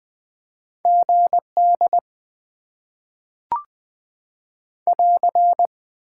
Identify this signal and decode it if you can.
Morse code, heard as a keyed tone, about 20 wpm, 700 Hz. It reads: GD <AR>
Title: Top 90 QSO Elements - 8x Character Spacing - 20wpm